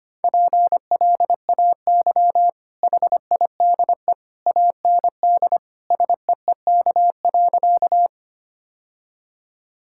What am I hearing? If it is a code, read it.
Morse code, heard as a keyed tone, about 25 wpm, 690 Hz. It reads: PLAY HIDE AND SEEK.
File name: Easy Sentences with Hints - 25wpm